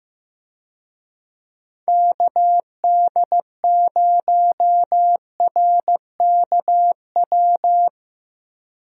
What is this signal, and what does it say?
Morse code, heard as a keyed tone, about 15 wpm, 700 Hz. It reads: KD0RKW
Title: USA Call Signs - 6 Characters - Repeated 3x - 15wpm